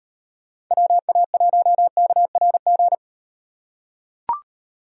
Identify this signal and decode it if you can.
Morse code, heard as a keyed tone, about 35 wpm, 700 Hz. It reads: WA1KRG